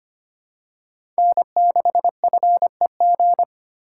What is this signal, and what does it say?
Morse code, heard as a keyed tone, about 25 wpm, 700 Hz. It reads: N6FEG
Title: USA Call Signs - 5 Characters - 4x Character Spacing - 25wpm